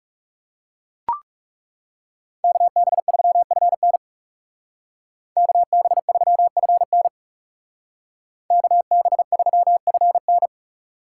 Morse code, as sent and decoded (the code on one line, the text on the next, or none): -.- -... ...-- ..-. -. / -.- -... ...-- ..-. -. / -.- -... ...-- ..-. -.
KB3FN KB3FN KB3FN